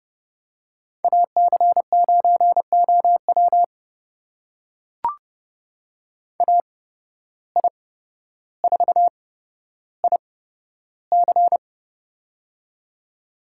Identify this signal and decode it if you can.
Morse code, heard as a keyed tone, about 30 wpm, spaced as for 13 wpm, 700 Hz. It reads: AC9OW AI4IC
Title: USA Call Signs - 5 Characters - 8x Character Spacing - 30wpm